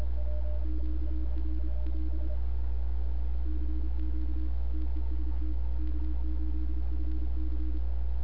Morse code, heard as a keyed tone, about 32 wpm, 310 Hz. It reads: QRL CQ TEST K8FC
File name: n6tr / XE1RCS 1025UTC